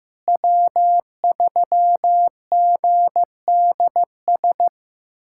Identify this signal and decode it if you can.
Morse code, heard as a keyed tone, about 15 wpm, 700 Hz. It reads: W3GDS
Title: USA Callsign DE Callsign 15wpm